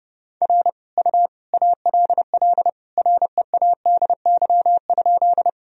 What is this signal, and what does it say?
Morse code, heard as a keyed tone, about 31 wpm, 700 Hz. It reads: R U ALL READY?
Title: Sentences from Top 400 Words and Abbreviations 30wpm